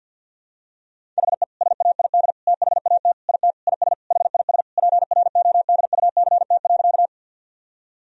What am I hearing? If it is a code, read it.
Morse code, heard as a keyed tone, about 52 wpm, 700 Hz. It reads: HE SAID THAT IT IS HIS PRODUCT.
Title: Sentences from Top 1000 Words - Review Part 2 - 50wpm